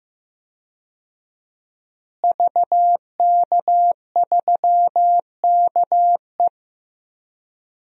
Morse code, heard as a keyed tone, about 15 wpm, 700 Hz. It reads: VK3KE
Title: DXCC-100 Call Signs - 5 Characters - Repeated 3x - 15wpm